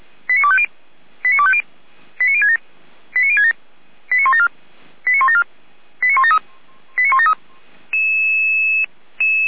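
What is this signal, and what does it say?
Five-tone selective call (ZVEI), heard as quick runs of five tones, tones of about 80 ms. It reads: 89250 89250 89067 89067 89174 89174 89173 89173
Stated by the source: ZVEI five-tone call received by SDR, logged by Feuerwehr Orlamünde (fire brigade).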